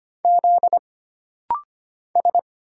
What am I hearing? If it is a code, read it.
Morse code, heard as a keyed tone, about 25 wpm, 700 Hz. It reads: Z S
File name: Single Character (Letter:Number:Punctuation:Prosign) - Rapid-Fire - 25wpm